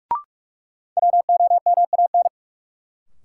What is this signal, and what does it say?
Morse code, heard as a keyed tone, about 46 wpm, 700 Hz. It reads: WOMAN
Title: Sentences from Top 1000 Words - Review Part 2 - New Word Drill - 45wpm